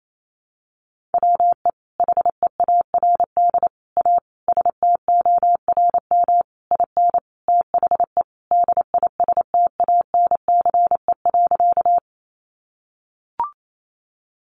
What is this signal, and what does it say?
Morse code, heard as a keyed tone, about 28 wpm, 700 Hz. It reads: WE HEARD A STORM IN THE DISTANCE.
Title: Sentences from Top 1500 Words 28wpm